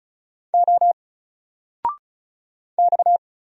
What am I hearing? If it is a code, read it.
Morse code, heard as a keyed tone, about 36 wpm, 700 Hz. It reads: O X
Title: Single Letter-Number 35wpm